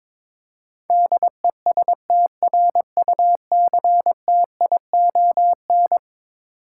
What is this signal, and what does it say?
Morse code, heard as a keyed tone, about 21 wpm, 700 Hz. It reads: DESTRUCTION
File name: practice / 22wpm / Eleven Letter Words - 22wpm